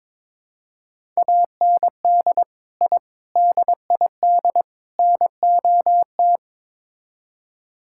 Morse code, heard as a keyed tone, about 21 wpm, 700 Hz. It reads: AND I DID NOT
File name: practice / 22wpm / Build A Phrase - 5-Word Phrases - 22wpm